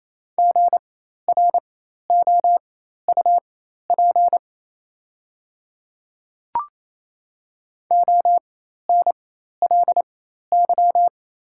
Morse code, heard as a keyed tone, about 27 wpm, spaced as for 13 wpm, 690 Hz. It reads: GROUP ONLY
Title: 4x Character Spacing - Top 500 Words - 28wpm